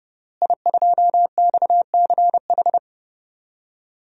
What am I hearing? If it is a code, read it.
Morse code, heard as a keyed tone, about 30 wpm, 700 Hz. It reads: I2XCH
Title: DXCC-100 Call Signs - 5 Characters - Repeated 3x - 30wpm